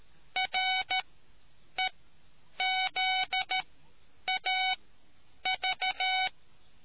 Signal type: Morse code